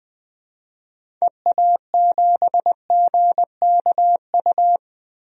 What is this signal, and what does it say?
Morse code, heard as a keyed tone, about 20 wpm, 700 Hz. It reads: EA7GKU
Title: DXCC-100 Call Signs - 6 Characters - 20wpm